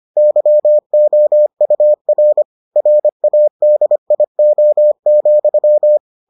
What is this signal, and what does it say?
Morse code, heard as a keyed tone, about 25 wpm, 600 Hz. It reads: YOUR RADIO,